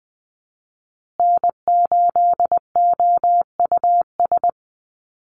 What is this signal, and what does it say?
Morse code, heard as a keyed tone, about 20 wpm, 700 Hz. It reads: N8OUS
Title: USA Call Signs Repeated 3x 20wpm